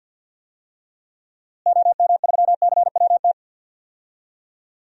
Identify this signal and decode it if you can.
Morse code, heard as a keyed tone, about 50 wpm, 700 Hz. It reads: OM3CWT
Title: DXCC-100 Call Signs - 6 Characters - Repeated 2x - 50wpm